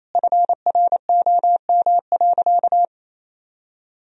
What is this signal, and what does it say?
Morse code, heard as a keyed tone, about 28 wpm, 700 Hz. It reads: FROM.
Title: Sentences from Top 1000 Words - Review Part 4 - 28wpm